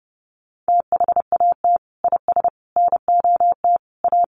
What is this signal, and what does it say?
Morse code, heard as a keyed tone, about 30 wpm, 700 Hz. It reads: THAT IS NOT A
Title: Sentences from Top 600 Words 30wpm